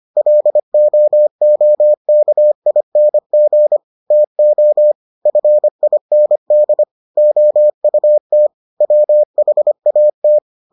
Morse code, heard as a keyed tone, about 25 wpm, 600 Hz. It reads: LOOKING TO FIND OUT WHAT